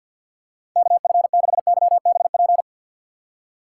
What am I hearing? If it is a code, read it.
Morse code, heard as a keyed tone, about 52 wpm, 700 Hz. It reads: KF6YBP
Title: USA Call Signs - 6 Characters - 4x Character Spacing - 50wpm